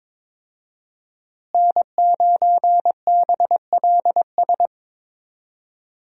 Morse code, heard as a keyed tone, about 22 wpm, 700 Hz. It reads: N9BLS
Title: USA Call Signs - 5 Characters - Repeated 2x - 22wpm